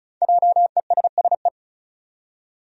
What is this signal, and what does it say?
Morse code, heard as a keyed tone, about 36 wpm, 700 Hz. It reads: JESSE
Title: USA Common Names - 35wpm